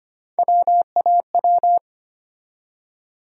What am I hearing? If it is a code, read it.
Morse code, heard as a keyed tone, about 25 wpm, 700 Hz. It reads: WAW
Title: USA Call Sign Postfixes - 3 Letters 25wpm